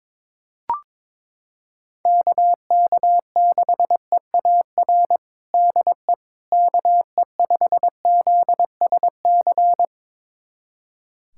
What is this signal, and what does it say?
Morse code, heard as a keyed tone, about 22 wpm, 700 Hz. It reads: KK6EAR DE KE5ZSC
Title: USA Callsign DE Callsign 22wpm